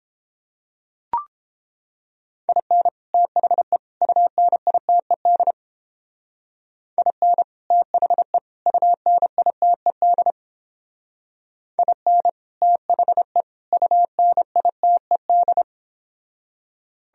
Morse code, heard as a keyed tone, about 31 wpm, 700 Hz. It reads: IN THE UNITED IN THE UNITED IN THE UNITED